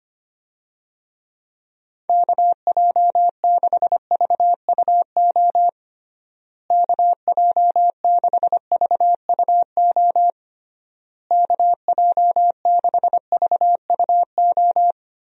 Morse code, heard as a keyed tone, about 25 wpm, 700 Hz. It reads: KJ6VUO KJ6VUO KJ6VUO
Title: USA Call Signs - 6 Characters - Repeated 3x - 25wpm